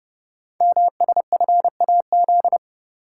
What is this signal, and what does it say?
Morse code, heard as a keyed tone, about 30 wpm, 700 Hz. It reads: MSFAZ
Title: Code Groups - 5 Letters - 30wpm